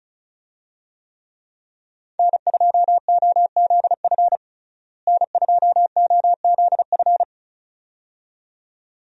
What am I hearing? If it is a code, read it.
Morse code, heard as a keyed tone, about 36 wpm, 700 Hz. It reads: N2OZF N2OZF
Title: USA Call Signs - 5 Characters - Repeated 2x - 35wpm